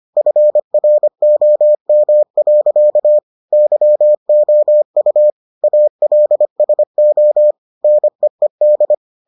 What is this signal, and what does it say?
Morse code, heard as a keyed tone, about 25 wpm, 600 Hz. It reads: FROM. YOU ALSO NEED